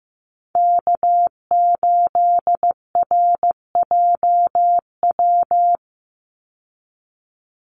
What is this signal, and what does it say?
Morse code, heard as a keyed tone, about 15 wpm, 700 Hz. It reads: K8RJW
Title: Call Signs Repeated 3x - 15wpm